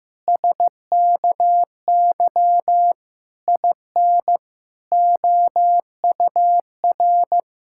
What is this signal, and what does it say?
Morse code, heard as a keyed tone, about 15 wpm, 700 Hz. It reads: SKY IN OUR
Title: Pangram Challenge v2 - 15wpm